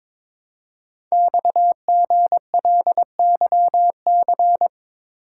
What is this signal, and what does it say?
Morse code, heard as a keyed tone, about 22 wpm, 700 Hz. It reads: XGLYC
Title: Code Groups - 5 Letters - 22wpm